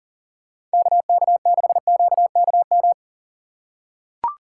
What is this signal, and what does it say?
Morse code, heard as a keyed tone, about 40 wpm, 700 Hz. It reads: KK6QKM
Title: USA Call Signs - 3x Character Spacing - 40wpm